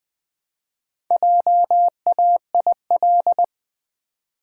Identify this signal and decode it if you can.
Morse code, heard as a keyed tone, about 21 wpm, 700 Hz. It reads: JAIL
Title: Build A Word - 1 Letter At A Time - 3 to 5-Letter Words - 20wpm